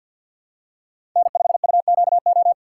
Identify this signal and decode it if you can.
Morse code, heard as a keyed tone, about 53 wpm, 690 Hz. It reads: N5UQO
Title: USA Call Signs - 2x Character Spacing - 50wpm